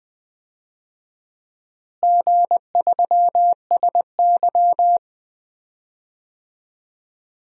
Morse code, heard as a keyed tone, about 21 wpm, 700 Hz. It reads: G3SY
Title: DXCC-100 Call Signs - 4 Characters - Repeated 2x - 20wpm